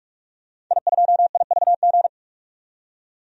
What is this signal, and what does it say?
Morse code, heard as a keyed tone, about 46 wpm, 700 Hz. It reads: I2IVG
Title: DXCC-100 Call Signs - 5 Characters - 45wpm